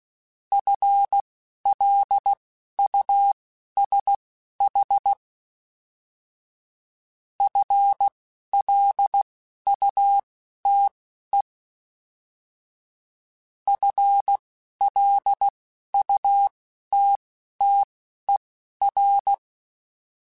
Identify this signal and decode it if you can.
Morse code, heard as a keyed tone, about 16 wpm, spaced as for 11 wpm, 800 Hz. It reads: FLUSH FLUTE FLUTTER
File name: FL Runs / FL Runs mp3s / FL Runs @10